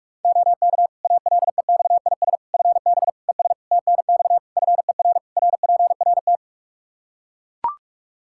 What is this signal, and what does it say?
Morse code, heard as a keyed tone, about 42 wpm, 700 Hz. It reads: OK ALEXIS FB ES TNX FER RPRT
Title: Practice QSO - Exchange 2 - Info - 45wpm